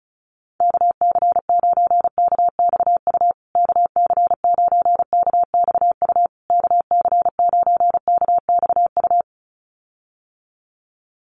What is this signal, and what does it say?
Morse code, heard as a keyed tone, about 34 wpm, 700 Hz. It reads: KC9KXU KC9KXU KC9KXU
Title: USA Call Signs Repeated 3x 35wpm